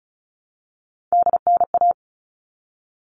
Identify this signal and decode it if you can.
Morse code, heard as a keyed tone, about 34 wpm, 700 Hz. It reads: DNA
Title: All Cut Numbers - 3-Digit 35wpm